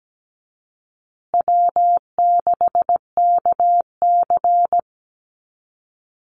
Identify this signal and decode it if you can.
Morse code, heard as a keyed tone, about 17 wpm, 700 Hz. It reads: W6KC
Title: Easy Call Signs Repeated 3x - 17wpm